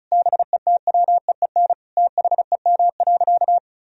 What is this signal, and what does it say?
Morse code, heard as a keyed tone, about 36 wpm, 700 Hz. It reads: BETWEEN THEM.
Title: Sentences from Top 400 Words and Abbreviations 35wpm